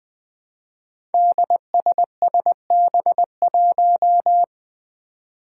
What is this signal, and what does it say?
Morse code, heard as a keyed tone, about 20 wpm, 700 Hz. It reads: DSSB1
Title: Five Challenging Letter-Number Combinations 20wpm